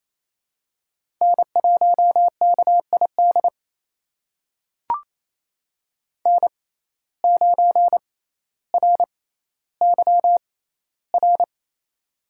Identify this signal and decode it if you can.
Morse code, heard as a keyed tone, about 28 wpm, spaced as for 14 wpm, 700 Hz. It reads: N1KID N9RYR
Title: USA Call Signs - 6x Character Spacing - 28wpm